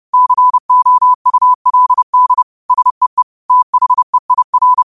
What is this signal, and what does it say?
Morse code, heard as a keyed tone, about 30 wpm, 1000 Hz. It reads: COULD SEE THEIR
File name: qotdfiles / 30wpm